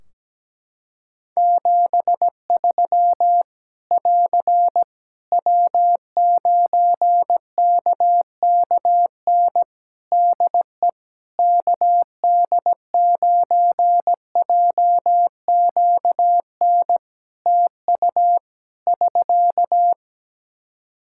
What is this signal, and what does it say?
Morse code, heard as a keyed tone, about 17 wpm, 700 Hz. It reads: 73 <AR> W9KKN DE KD9JQN TU <SK>
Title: Practice QSO - Exchange 4 - Ending - 17wpm